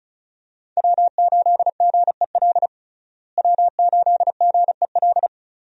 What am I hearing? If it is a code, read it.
Morse code, heard as a keyed tone, about 36 wpm, 700 Hz. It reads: W8GEL W8GEL
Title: USA Call Signs - 5 Characters - Repeated 2x - 35wpm